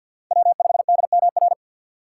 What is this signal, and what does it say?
Morse code, heard as a keyed tone, about 51 wpm, 700 Hz. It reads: W5DMR
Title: USA Call Signs - 4x Character Spacing - 50wpm